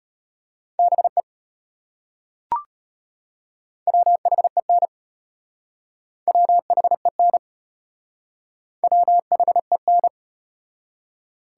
Morse code, heard as a keyed tone, about 33 wpm, 700 Hz. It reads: BE WHEN WHEN WHEN